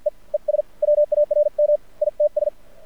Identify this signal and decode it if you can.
Morse code, heard as a keyed tone, about 52 wpm, spaced as for 42 wpm, 600 Hz. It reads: E E SWARM ITS
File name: archived / CRSnet